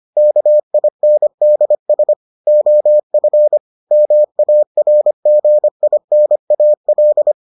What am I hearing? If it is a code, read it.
Morse code, heard as a keyed tone, about 25 wpm, 600 Hz. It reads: KINDS OF MARGINAL